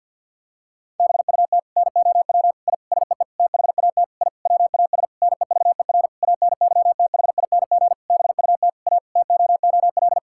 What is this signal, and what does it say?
Morse code, heard as a keyed tone, about 52 wpm, 700 Hz. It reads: BUT NOW I SEE THAT I WAS NEVER ANYTHING BUT A TOOL
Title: Sentences from Top 1000 Words - Review Part 6 - 50wpm